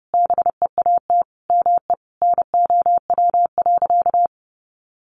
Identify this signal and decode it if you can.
Morse code, heard as a keyed tone, about 30 wpm, 700 Hz. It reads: BEAT ME NOW.
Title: Sentences from Top 1300 Words 30wpm